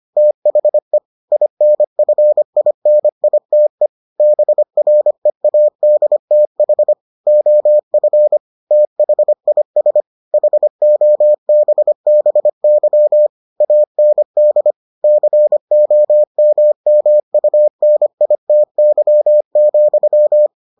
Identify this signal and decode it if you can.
Morse code, heard as a keyed tone, about 25 wpm, 600 Hz. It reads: THE INFINITE BREADTH OF THIS HOBBY AND COMMUNITY,